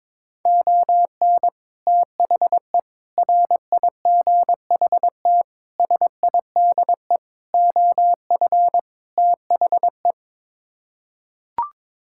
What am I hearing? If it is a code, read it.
Morse code, heard as a keyed tone, about 22 wpm, 700 Hz. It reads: ON THE RIGHT SIDE OF THE
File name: practice / 22wpm / Sets of 6 Words - 22wpm